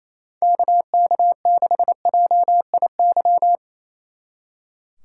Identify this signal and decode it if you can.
Morse code, heard as a keyed tone, about 32 wpm, 700 Hz. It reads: KK6JIY
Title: Speed Racing - USA 2x3 Call Signs 28wpm